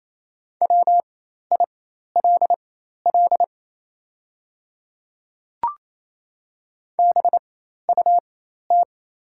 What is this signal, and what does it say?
Morse code, heard as a keyed tone, about 28 wpm, spaced as for 13 wpm, 700 Hz. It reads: WILL BUT
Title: 4x Character Spacing - Top 100 Words - 28wpm